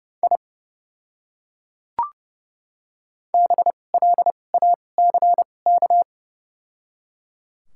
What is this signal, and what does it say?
Morse code, heard as a keyed tone, about 32 wpm, spaced as for 22 wpm, 700 Hz. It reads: I BLACK
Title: 2x Character Spacing - Top 500 Words - 30wpm